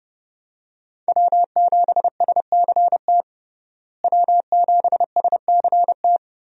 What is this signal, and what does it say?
Morse code, heard as a keyed tone, about 30 wpm, 700 Hz. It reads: W7SCT W7SCT
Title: USA Call Signs - 5 Characters - Repeated 2x - 30wpm